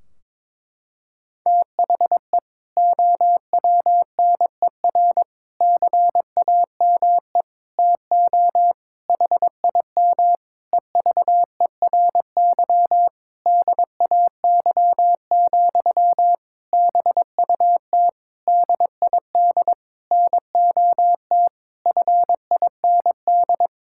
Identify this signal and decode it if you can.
Morse code, heard as a keyed tone, about 22 wpm, 700 Hz. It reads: THE OWNER CAME TO HIM EVERY DAY, BUT DID NOT FIND
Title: Sentences from Top 1600 Words 22wpm